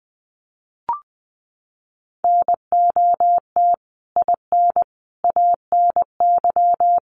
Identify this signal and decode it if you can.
Morse code, heard as a keyed tone, about 20 wpm, 700 Hz. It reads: NOT IN ANY
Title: Sentences from Top 1500 Words 20wpm